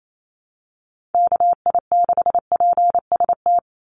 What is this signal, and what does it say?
Morse code, heard as a keyed tone, about 28 wpm, 700 Hz. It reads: KI6PST
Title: USA Call Signs 28wpm